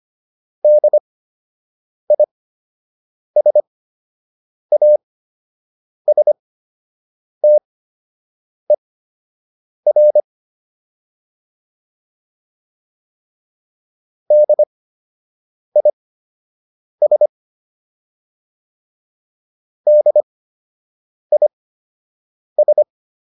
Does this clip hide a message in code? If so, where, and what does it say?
Yes: Morse code from 0.6 to 22.8 s, heard as a keyed tone, about 25 wpm, spaced as for 7 wpm, 600 Hz. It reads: DISASTER DIS DIS